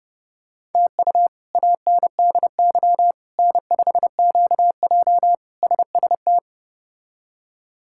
Morse code, heard as a keyed tone, about 30 wpm, 700 Hz. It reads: TU ANDY N5QJ SST
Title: K1USN SST QSO Practice - 30wpm